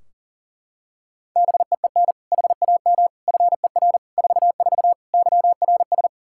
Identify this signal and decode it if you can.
Morse code, heard as a keyed tone, about 40 wpm, 700 Hz. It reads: BEEN HAM FER 44 YRS
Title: Practice QSO - Exchange 3 - More Info - 40wpm